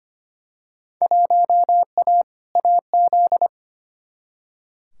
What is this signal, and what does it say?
Morse code, heard as a keyed tone, about 25 wpm, 700 Hz. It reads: 1A AZ
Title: Field Day Exchanges - 25wpm